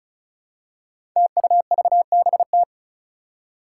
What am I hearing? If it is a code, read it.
Morse code, heard as a keyed tone, about 36 wpm, 700 Hz. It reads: TUVBT
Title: Five Challenging Letter-Number Combinations 35wpm